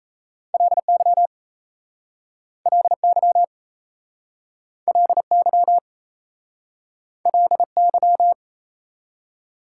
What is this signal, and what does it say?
Morse code, heard as a keyed tone, about 38 wpm, 700 Hz. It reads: LY LY LY LY